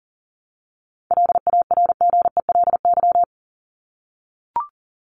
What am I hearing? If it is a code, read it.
Morse code, heard as a keyed tone, about 40 wpm, 700 Hz. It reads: LARGELY
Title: LY Suffix - Top 100 Words with Base Words 40wpm